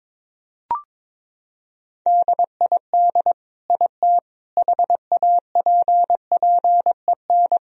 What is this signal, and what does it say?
Morse code, heard as a keyed tone, about 22 wpm, 700 Hz. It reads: DID IT HAPPEN
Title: Sentences from Top 1000 Words - Review Part 1 - 22wpm